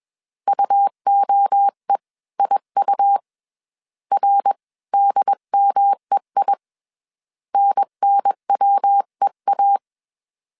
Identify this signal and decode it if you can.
Morse code, heard as a keyed tone, about 21 wpm, spaced as for 18 wpm, 800 Hz. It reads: UOEIU RDMEI NNWEA
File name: CW / DET / 1 3 1500C S B 20wpm 1754